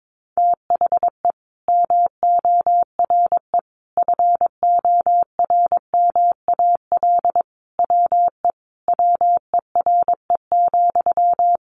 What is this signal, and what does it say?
Morse code, heard as a keyed tone, about 22 wpm, 700 Hz. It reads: THE MORE FORMAL WE WERE,
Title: Sentences from Top 1300 Words 22wpm